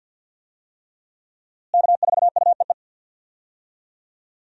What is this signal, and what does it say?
Morse code, heard as a keyed tone, about 52 wpm, 700 Hz. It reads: K4UEE
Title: USA Call Signs - 5 Characters - Repeated 3x - 50wpm